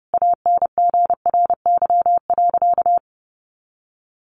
Morse code, heard as a keyed tone, about 30 wpm, 700 Hz. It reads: ANGRY.
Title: Easy Phrases 30wpm